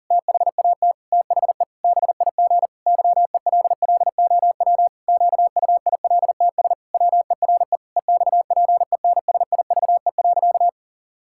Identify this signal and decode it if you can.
Morse code, heard as a keyed tone, about 40 wpm, 700 Hz. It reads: THAT THE BIG YELLOW QUILTS WERE EXPENSIVE.
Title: Pangram Challenge v2 - 40wpm